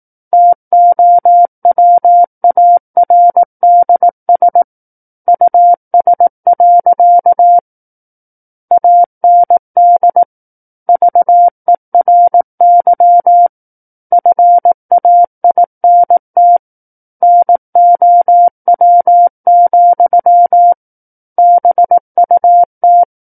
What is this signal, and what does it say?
Morse code, heard as a keyed tone, about 18 wpm, 700 Hz. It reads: TOWARDS US. AND VERY FAINT NOW, BUT
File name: Book One / 09-Chapter 9 - 18 WPM